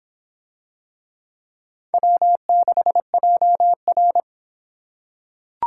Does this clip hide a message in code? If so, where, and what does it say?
Yes: Morse code from 1.9 to 4.2 s, heard as a keyed tone, about 24 wpm, 700 Hz. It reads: W6JR